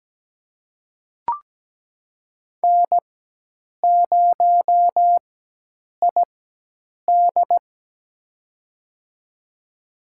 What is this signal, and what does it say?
Morse code, heard as a keyed tone, about 17 wpm, spaced as for 8 wpm, 700 Hz. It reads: N0ID
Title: USA Call Signs - 4 Characters - 4x Character Spacing - 17wpm